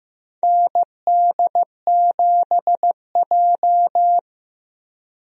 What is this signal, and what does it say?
Morse code, heard as a keyed tone, about 15 wpm, 700 Hz. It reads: ND7J
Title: USA Call Signs - 4 Characters - Repeated 3x - 15wpm